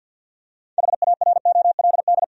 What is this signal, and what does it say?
Morse code, heard as a keyed tone, about 52 wpm, 700 Hz. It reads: HAROLD
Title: USA Common Names - 50wpm